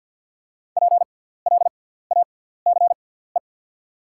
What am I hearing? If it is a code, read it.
Morse code, heard as a keyed tone, about 48 wpm, spaced as for 17 wpm, 690 Hz. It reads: PLACE
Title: 6x Character Spacing - Top 200 Words - 50wpm